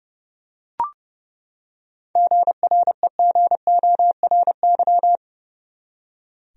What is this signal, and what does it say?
Morse code, heard as a keyed tone, about 30 wpm, 700 Hz. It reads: GREGORY
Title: USA Common Names - 30wpm